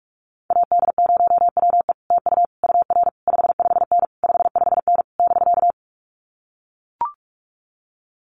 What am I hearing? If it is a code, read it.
Morse code, heard as a keyed tone, about 44 wpm, 700 Hz. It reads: AD0WE TU UR 55N 55N <BK>
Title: Common QSO Phrases Encore 45wpm